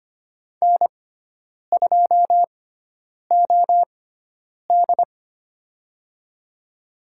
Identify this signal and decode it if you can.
Morse code, heard as a keyed tone, about 25 wpm, spaced as for 9 wpm, 700 Hz. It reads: N2OD
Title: USA Call Signs - 6x Character Spacing - 25wpm